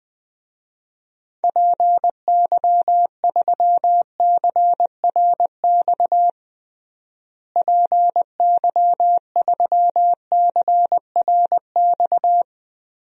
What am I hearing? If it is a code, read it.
Morse code, heard as a keyed tone, about 21 wpm, 700 Hz. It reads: PY3CRX PY3CRX
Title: DXCC-100 Call Signs - 6 Characters - Repeated 3x - 20wpm